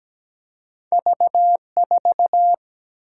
Morse code, heard as a keyed tone, about 17 wpm, 700 Hz. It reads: V4
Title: Single-Letter Single-Number 17wpm